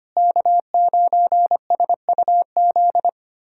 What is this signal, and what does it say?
Morse code, heard as a keyed tone, about 25 wpm, 700 Hz. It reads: K9SUZ
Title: USA Callsign DE Callsign - 5 Characters - 25wpm